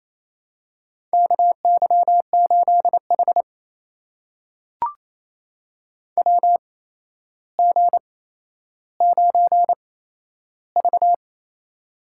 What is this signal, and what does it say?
Morse code, heard as a keyed tone, about 28 wpm, spaced as for 12 wpm, 700 Hz. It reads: KY8H WG9V
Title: USA Call Signs - 4 Characters - 8x Character Spacing - 28wpm